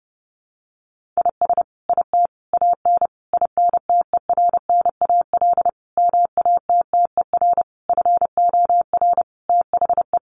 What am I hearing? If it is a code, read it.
Morse code, heard as a keyed tone, about 30 wpm, 700 Hz. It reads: IS IT AN INTERNAL MATTER FOR THE
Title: Sentences from Top 1400 Words 30wpm